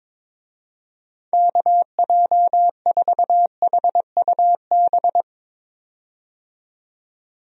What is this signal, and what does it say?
Morse code, heard as a keyed tone, about 22 wpm, 700 Hz. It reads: KJ4HUB
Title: USA Call Signs - 6 Characters - 22wpm